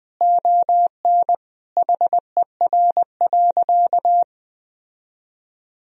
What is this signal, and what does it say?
Morse code, heard as a keyed tone, about 21 wpm, 700 Hz. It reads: ON HER.
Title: Sentences from Top 1000 Words - Review Part 1 - 20wpm